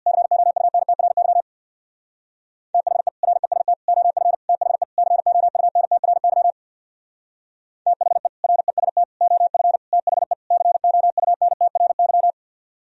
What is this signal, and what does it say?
Morse code, heard as a keyed tone, about 64 wpm, 700 Hz. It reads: COUNTRY THE REST OF THE COUNTRY THE REST OF THE COUNTRY